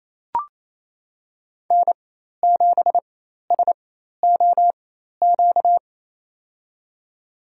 Morse code, heard as a keyed tone, about 28 wpm, spaced as for 13 wpm, 700 Hz. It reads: N7SOQ
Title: USA Call Signs - 4x Character Spacing - 28wpm